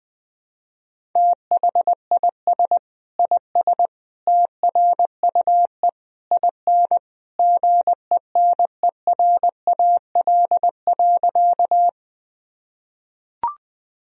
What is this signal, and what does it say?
Morse code, heard as a keyed tone, about 21 wpm, 700 Hz. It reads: THIS IS TRUE IN GENERAL.
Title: Sentences from Top 1000 Words - Review Part 3 - 20wpm